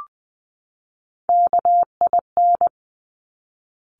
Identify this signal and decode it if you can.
Morse code, heard as a keyed tone, about 20 wpm, 700 Hz. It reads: KIN
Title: Top Three Letter Combinations in Words 20wpm